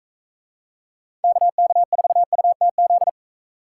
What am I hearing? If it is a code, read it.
Morse code, heard as a keyed tone, about 41 wpm, 700 Hz. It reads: KK4UTZ